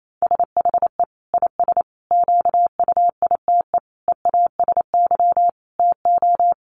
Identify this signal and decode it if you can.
Morse code, heard as a keyed tone, about 28 wpm, 700 Hz. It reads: SHE IS QUITE EASY TO
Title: Sentences from Top 1500 Words 28wpm